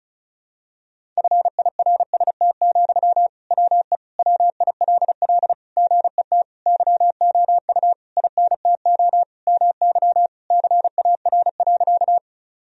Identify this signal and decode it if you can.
Morse code, heard as a keyed tone, about 34 wpm, 700 Hz. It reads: FIRST, WE WILL GET YOU INTO MY CAR.
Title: Sentences from Top 1000 Words - Review Part 1 - 35wpm